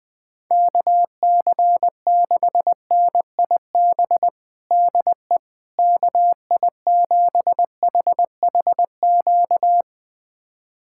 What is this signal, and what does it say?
Morse code, heard as a keyed tone, about 21 wpm, 700 Hz. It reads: KC6NIB DE KI7HHQ
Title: USA Callsign DE Callsign - 6 Characters - 20wpm